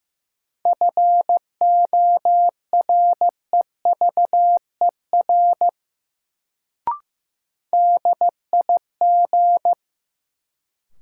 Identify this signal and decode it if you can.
Morse code, heard as a keyed tone, about 15 wpm, 700 Hz. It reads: FOREVER DIG
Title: Sentences from Top 1600 Words - New Word Drill Alt - 15wpm